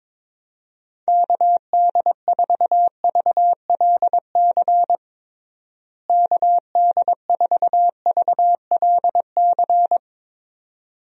Morse code, heard as a keyed tone, about 22 wpm, 700 Hz. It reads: KD4VLC KD4VLC
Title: USA Call Signs - 6 Characters - Repeated 2x - 22wpm